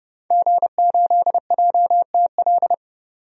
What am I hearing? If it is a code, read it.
Morse code, heard as a keyed tone, about 30 wpm, 700 Hz. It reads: G8JTL
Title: DXCC-100 Call Signs - 5 Characters - 30wpm